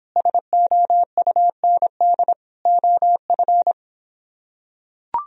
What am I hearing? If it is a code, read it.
Morse code, heard as a keyed tone, about 26 wpm, 700 Hz. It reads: SOUND OF